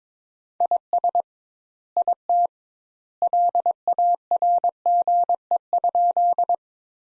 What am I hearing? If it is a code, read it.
Morse code, heard as a keyed tone, about 22 wpm, 700 Hz. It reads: IS IT LARGE?
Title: Sentences Top 200 Words - 22wpm